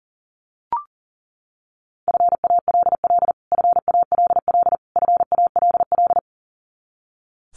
Morse code, heard as a keyed tone, about 40 wpm, 700 Hz. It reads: FALL FALL FALL
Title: Top 500 Words Repeated 40wpm